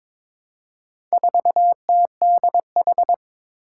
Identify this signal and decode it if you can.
Morse code, heard as a keyed tone, about 22 wpm, 700 Hz. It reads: 4TDH
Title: Four Challenging Letter-Number Combinations 22wpm